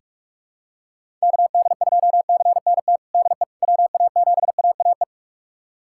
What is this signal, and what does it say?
Morse code, heard as a keyed tone, about 45 wpm, 700 Hz. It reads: KD2CNT DE WA7AAE
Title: USA Callsign DE Callsign - 6 Characters - 45wpm